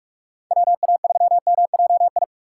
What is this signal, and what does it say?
Morse code, heard as a keyed tone, about 59 wpm, spaced as for 52 wpm, 700 Hz. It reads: WA3MJI
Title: Speed Racing - USA 2x3 Call Signs 45wpm